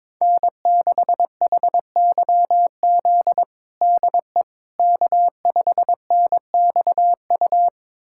Morse code, heard as a keyed tone, about 22 wpm, 700 Hz. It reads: N6HYZ DE K5NXU
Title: USA Callsign DE Callsign - 5 Characters - 22wpm